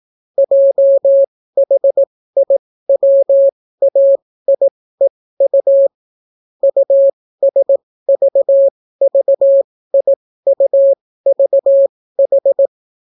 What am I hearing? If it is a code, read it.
Morse code, heard as a keyed tone, about 18 wpm, spaced as for 14 wpm, 550 Hz. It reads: JHIWAIEU USVVIUVH